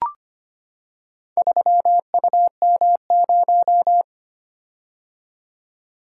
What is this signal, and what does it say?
Morse code, heard as a keyed tone, about 25 wpm, 700 Hz. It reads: 3UM0
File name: practice / 25wpm / Four Letter-Number Combinations - 25wpm